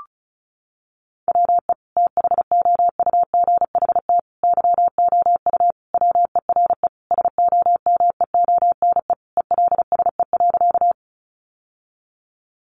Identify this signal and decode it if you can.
Morse code, heard as a keyed tone, about 34 wpm, 700 Hz. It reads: WE THOUGHT YOU WERE SOMEONE ELSE.
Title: Sentences from Top 800 Words 35wpm